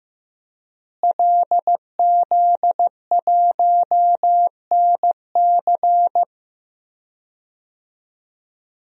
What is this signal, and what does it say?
Morse code, heard as a keyed tone, about 15 wpm, 700 Hz. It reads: LZ1NC
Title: DXCC-100 Call Signs - 5 Characters - Repeated 2x - 15wpm